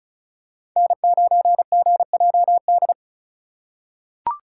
Morse code, heard as a keyed tone, about 36 wpm, 700 Hz. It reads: N9GJD
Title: USA Call Signs - 5 Characters - 2x Character Spacing - 35wpm